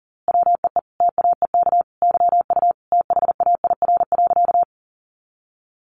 Morse code, heard as a keyed tone, about 40 wpm, 700 Hz. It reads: WEE TAEK YU THAIR.
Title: A Sound Of Thunder 40wpm